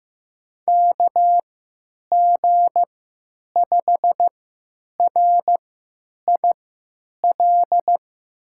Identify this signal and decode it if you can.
Morse code, heard as a keyed tone, about 15 wpm, spaced as for 9 wpm, 700 Hz. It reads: KG5RIL
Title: USA Call Signs - 6 Characters - 3x Character Spacing - 15wpm